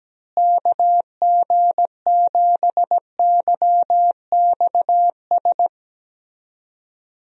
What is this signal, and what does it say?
Morse code, heard as a keyed tone, about 17 wpm, 700 Hz. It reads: KG7YXS
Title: USA Call Signs - 6 Characters - Repeated 3x - 17wpm